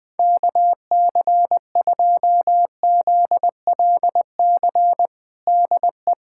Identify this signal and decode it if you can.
Morse code, heard as a keyed tone, about 20 wpm, 700 Hz. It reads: KC2ZLC DE
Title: USA Callsign DE Callsign 20wpm